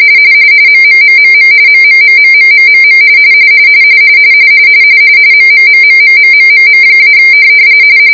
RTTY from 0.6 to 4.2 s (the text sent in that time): MMMMM MMMMMMMM